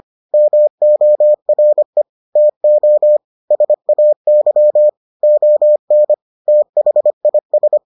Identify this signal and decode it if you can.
Morse code, heard as a keyed tone, about 25 wpm, 600 Hz. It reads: MORE TO SAY ON THIS